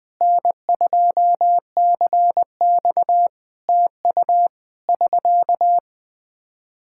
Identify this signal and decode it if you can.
Morse code, heard as a keyed tone, about 20 wpm, 700 Hz. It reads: N2CX TU <SK>
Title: Practice QSO - Exchange 4 - Ending - 20wpm